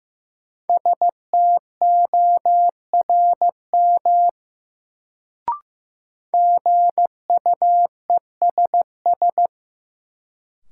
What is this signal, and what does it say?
Morse code, heard as a keyed tone, about 16 wpm, 700 Hz. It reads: STORM GUESS
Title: Sentences from Top 1500 Words - New Word Drill - 15wpm